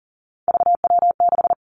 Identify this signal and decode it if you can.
Morse code, heard as a keyed tone, about 40 wpm, 700 Hz. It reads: VW6
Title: Three Challenging Letter-Number Combinations 40wpm